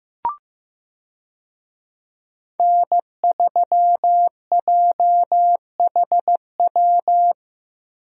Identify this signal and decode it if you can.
Morse code, heard as a keyed tone, about 15 wpm, 700 Hz. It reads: N3JHW
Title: USA Call Signs - 5 Characters - Repeated 2x - 15wpm